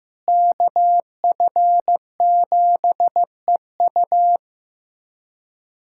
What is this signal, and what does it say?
Morse code, heard as a keyed tone, about 15 wpm, 700 Hz. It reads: KF7EU
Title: USA Callsign DE Callsign - 5 Characters - 15wpm